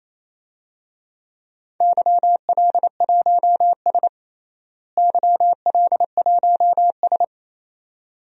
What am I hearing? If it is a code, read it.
Morse code, heard as a keyed tone, about 28 wpm, 700 Hz. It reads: YL1S YL1S
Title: DXCC-100 Call Signs - 4 Characters - Repeated 2x - 28wpm